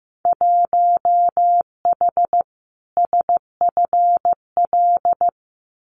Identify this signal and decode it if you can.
Morse code, heard as a keyed tone, about 15 wpm, 700 Hz. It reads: 1H SFL
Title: Winter Field Day Exchanges 15wpm